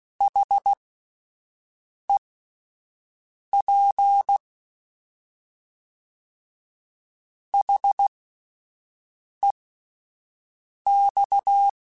Morse code, heard as a keyed tone, about 16 wpm, spaced as for 5 wpm, 800 Hz. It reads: HEP HEX